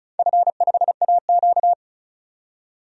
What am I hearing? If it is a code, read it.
Morse code, heard as a keyed tone, about 34 wpm, 700 Hz. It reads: F5AQ
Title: DXCC-100 Call Signs - 4 Characters - 35wpm